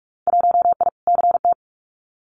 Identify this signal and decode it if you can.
Morse code, heard as a keyed tone, about 46 wpm, 700 Hz. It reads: 1I CT
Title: Winter Field Day Exchanges 45wpm